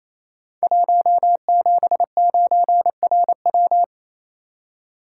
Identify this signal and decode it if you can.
Morse code, heard as a keyed tone, about 28 wpm, 700 Hz. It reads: 179RW
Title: Code Groups - 5 Letter:Numbers - 28wpm